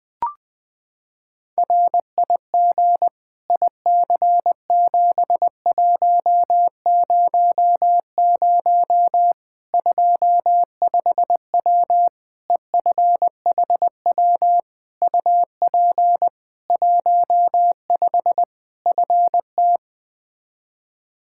Morse code, heard as a keyed tone, about 20 wpm, 700 Hz. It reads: RIG IC7100 25W EFHW UP 15 FT
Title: Practice QSO - Exchange 2 - Info - 20wpm